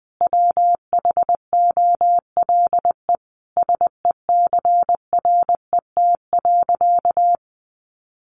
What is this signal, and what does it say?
Morse code, heard as a keyed tone, about 20 wpm, 700 Hz. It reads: WHOLE SECRET.
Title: Sentences from Top 1200 Words 20wpm